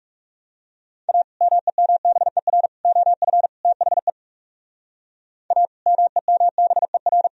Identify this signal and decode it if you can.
Morse code, heard as a keyed tone, about 40 wpm, 700 Hz. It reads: A MEMBER OF THE A MEMBER